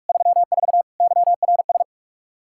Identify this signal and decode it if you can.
Morse code, heard as a keyed tone, about 42 wpm, 700 Hz. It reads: 34 YRS
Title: Practice QSO - Exchange 3 - More Info - 45wpm